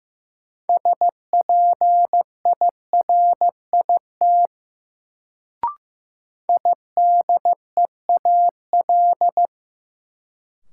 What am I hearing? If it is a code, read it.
Morse code, heard as a keyed tone, about 15 wpm, 700 Hz. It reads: SPIRIT IDEAL
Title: Sentences from Top 1400 Words - New Word Drill - 15wpm